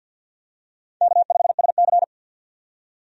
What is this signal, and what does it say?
Morse code, heard as a keyed tone, about 52 wpm, 700 Hz. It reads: K5SC
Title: USA Call Signs - 4 Characters - 50wpm